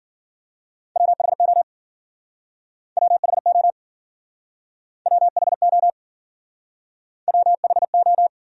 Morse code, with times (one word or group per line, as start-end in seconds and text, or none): 1.0-1.6 WHO
3.0-3.7 WHO
5.1-5.9 WHO
7.3-8.3 WHO